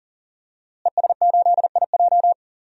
Morse code, heard as a keyed tone, about 40 wpm, 700 Hz. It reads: ES8IJ
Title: DXCC-100 Call Signs - 5 Characters - Repeated 2x - 40wpm